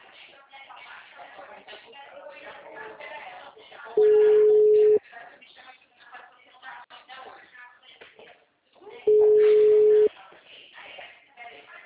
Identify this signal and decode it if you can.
Ringback tone (Europe, 425 Hz), with speech.